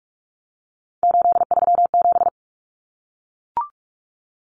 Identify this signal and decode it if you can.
Morse code, heard as a keyed tone, about 44 wpm, 700 Hz. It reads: 837
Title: Numbers - 3-Digit 45wpm